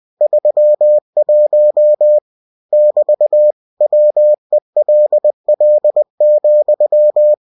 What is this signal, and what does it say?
Morse code, heard as a keyed tone, about 20 wpm, 600 Hz. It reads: 31 <BT> WELL,
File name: winnie-20wpm